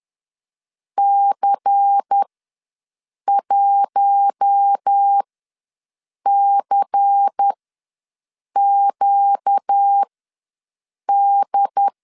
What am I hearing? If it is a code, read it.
Morse code, heard as a keyed tone, about 10 wpm, spaced as for 6 wpm, 800 Hz. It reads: C1CQD
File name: LEZIONI / 5 LBVC / 5 LBVC ws 10wpm 3733